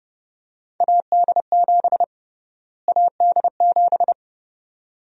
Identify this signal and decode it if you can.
Morse code, heard as a keyed tone, about 31 wpm, 700 Hz. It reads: AD7 AD7
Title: USA Call Sign Prefixes - Repeated 2x - 30wpm